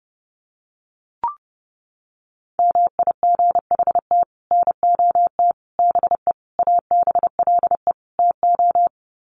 Morse code, heard as a keyed tone, about 30 wpm, 700 Hz. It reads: MIGHT NOT BE ABLE TO
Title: Sets of 5 Words Encore2 30wpm